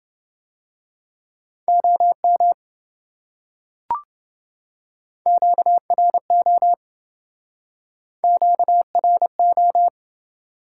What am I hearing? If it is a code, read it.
Morse code, heard as a keyed tone, about 27 wpm, 700 Hz. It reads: OM QRO QRO